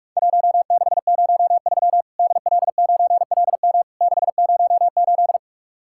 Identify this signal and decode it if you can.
Morse code, heard as a keyed tone, about 44 wpm, 700 Hz. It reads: 1603 DL9LM 608
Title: Worked All Europe DX Contest - QTC Exchange - 45wpm